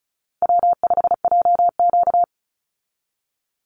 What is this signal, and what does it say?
Morse code, heard as a keyed tone, about 34 wpm, 700 Hz. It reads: W5JQ
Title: Call Signs Repeated 3x - 35wpm